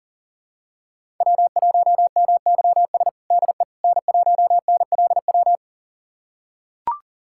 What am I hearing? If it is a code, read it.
Morse code, heard as a keyed tone, about 40 wpm, 700 Hz. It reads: W1MYS DE N1NLW
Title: USA Callsign DE Callsign - 5 Characters - 40wpm